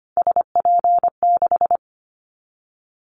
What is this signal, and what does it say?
Morse code, heard as a keyed tone, about 25 wpm, 700 Hz. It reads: SP6
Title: Callsign Prefixes for Top 25 DXCC - Repeated 3x - 25wpm